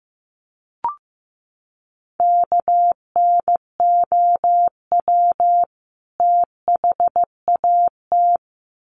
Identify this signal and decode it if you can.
Morse code, heard as a keyed tone, about 15 wpm, 700 Hz. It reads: KNOW THAT
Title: Sets of 2 Words 15wpm